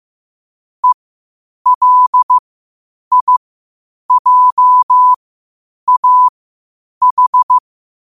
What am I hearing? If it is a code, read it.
Morse code, heard as a keyed tone, about 15 wpm, spaced as for 8 wpm, 1000 Hz. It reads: ELIJAH
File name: qotdfiles / 8wpm